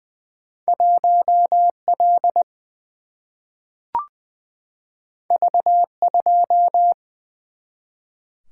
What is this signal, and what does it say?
Morse code, heard as a keyed tone, about 20 wpm, 700 Hz. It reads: 1 L V 2
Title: Single-Letter Single-Number 20wpm